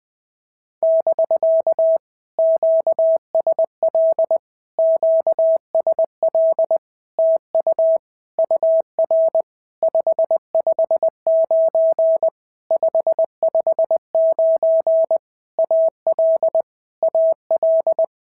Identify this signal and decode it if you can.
Morse code, heard as a keyed tone, about 19 wpm, 650 Hz. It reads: <BK> QSL QSL TU UR 559 559 AL AL
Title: POTA QSO Practice - 20wpm